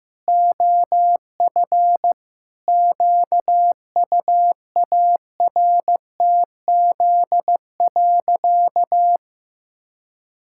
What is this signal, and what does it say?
Morse code, heard as a keyed tone, about 15 wpm, 700 Hz. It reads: OF QUARTZ.
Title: Pangram Challenge v2 - 15wpm